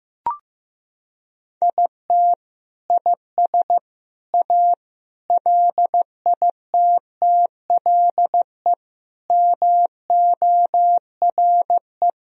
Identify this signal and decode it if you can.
Morse code, heard as a keyed tone, about 15 wpm, 700 Hz. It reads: IT IS A LITTLE MORE
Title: Sentences from Top 1000 Words - Review Part 1 - 15wpm